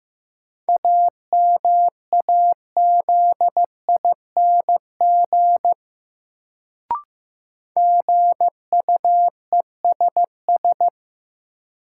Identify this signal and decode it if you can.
Morse code, heard as a keyed tone, about 15 wpm, 700 Hz. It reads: AMAZING GUESS
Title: Sentences from Top 1500 Words - New Word Drill - 15wpm